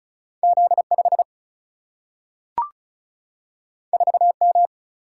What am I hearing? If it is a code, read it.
Morse code, heard as a keyed tone, about 36 wpm, 700 Hz. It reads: Z5 4M
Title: Single-Letter Single-Number 35wpm